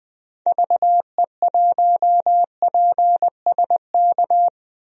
Morse code, heard as a keyed tone, about 20 wpm, 700 Hz. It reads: VE1PSK
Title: DXCC-100 Call Signs - 6 Characters - 20wpm